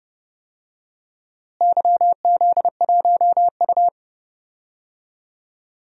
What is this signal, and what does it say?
Morse code, heard as a keyed tone, about 30 wpm, 700 Hz. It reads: YZ1U
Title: DXCC-100 Call Signs - 4 Characters - Repeated 2x - 30wpm